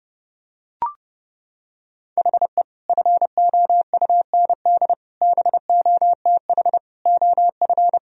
Morse code, heard as a keyed tone, about 30 wpm, 700 Hz. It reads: HE FOUND BOTH OF
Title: Sentences from Top 1000 Words - Review Part 1 - 30wpm